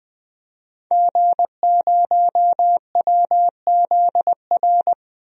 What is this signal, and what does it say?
Morse code, heard as a keyed tone, about 20 wpm, 700 Hz. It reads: G0WZR
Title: DXCC-100 Call Signs - 5 Characters - 20wpm